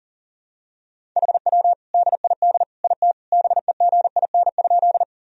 Morse code, heard as a keyed tone, about 40 wpm, 700 Hz. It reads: HW DID IT BEGIN?
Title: Sentences from Top 300 Words and Abbreviations 40wpm